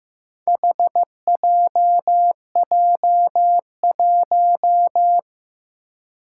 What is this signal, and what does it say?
Morse code, heard as a keyed tone, about 15 wpm, 700 Hz. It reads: HJJ1
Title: Four Challenging Letter-Number Combinations 15wpm